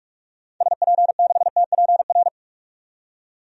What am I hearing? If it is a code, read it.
Morse code, heard as a keyed tone, about 46 wpm, 700 Hz. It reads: SP6TPR
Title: DXCC-100 Call Signs - 6 Characters - 45wpm